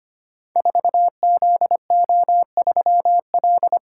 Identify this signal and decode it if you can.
Morse code, heard as a keyed tone, about 25 wpm, 700 Hz. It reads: 4ZO3L
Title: Code Groups - 5 Letter:Numbers - 25wpm